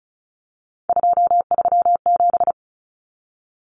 Morse code, heard as a keyed tone, about 34 wpm, 700 Hz. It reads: 237
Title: Numbers - 3-Digit 35wpm